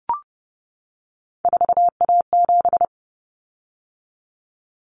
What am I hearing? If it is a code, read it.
Morse code, heard as a keyed tone, about 30 wpm, 700 Hz. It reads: 4A7
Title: Common Cut Numbers - 3-Digit 30wpm